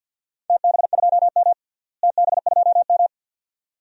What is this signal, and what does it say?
Morse code, heard as a keyed tone, about 52 wpm, 700 Hz. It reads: TB2M TB2M
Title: DXCC-100 Call Signs - 4 Characters - Repeated 2x - 50wpm